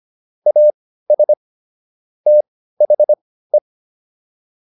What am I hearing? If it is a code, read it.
Morse code, heard as a keyed tone, about 25 wpm, spaced as for 15 wpm, 600 Hz. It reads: AS THE